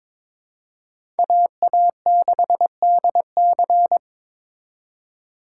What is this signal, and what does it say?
Morse code, heard as a keyed tone, about 22 wpm, 700 Hz. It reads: AA6DC
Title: USA Call Signs - 5 Characters - 2x Character Spacing - 22wpm